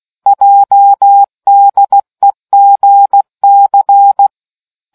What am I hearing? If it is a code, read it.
Morse code, heard as a keyed tone, about 16 wpm, 800 Hz. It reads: JDEGC